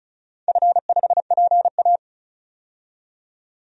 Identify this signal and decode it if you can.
Morse code, heard as a keyed tone, about 34 wpm, 700 Hz. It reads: F5PA
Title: DXCC-100 Call Signs - 4 Characters - 35wpm